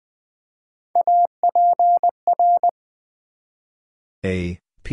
Morse code, heard as a keyed tone, about 20 wpm, 700 Hz. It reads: APR
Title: Build A Word - 1 Letter At A Time - 5-Letter Words - 20wpm